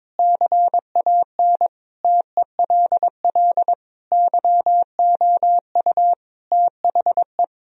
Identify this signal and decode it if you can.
Morse code, heard as a keyed tone, about 22 wpm, 700 Hz. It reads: CAN TELL YOU THE
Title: Sentences from Top 1000 Words - Review Part 2 - 22wpm